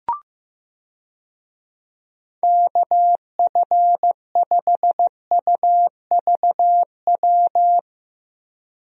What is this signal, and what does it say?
Morse code, heard as a keyed tone, about 15 wpm, 700 Hz. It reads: KF5UVW
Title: USA Call Signs - 6 Characters - Repeated 2x - 15wpm